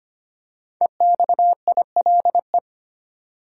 Morse code, heard as a keyed tone, about 25 wpm, 700 Hz. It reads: EXILE
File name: 5 Letter Words - First 3 Letters Given - 25wpm